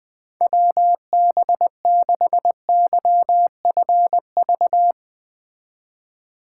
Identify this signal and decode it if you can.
Morse code, heard as a keyed tone, about 21 wpm, 700 Hz. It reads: WB6YFV